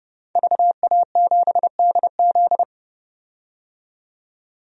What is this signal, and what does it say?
Morse code, heard as a keyed tone, about 30 wpm, 700 Hz. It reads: VA7DZ
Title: DXCC-100 Call Signs - 5 Characters - 30wpm